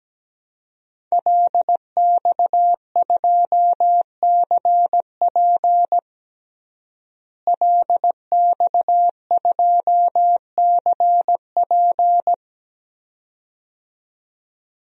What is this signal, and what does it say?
Morse code, heard as a keyed tone, about 17 wpm, 700 Hz. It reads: LX2CP LX2CP
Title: DXCC-100 Call Signs - 5 Characters - Repeated 2x - 17wpm